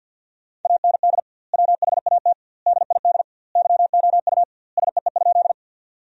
Morse code, heard as a keyed tone, about 50 wpm, 700 Hz. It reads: AND WHAT DID YOU SEE?
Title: 2x Word-Spacing - Sentences from Top 100 Words - 50wpm